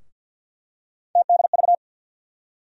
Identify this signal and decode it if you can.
Morse code, heard as a keyed tone, about 52 wpm, 700 Hz. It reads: TDV
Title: USA Call Sign Postfixes - 3 Letters 50wpm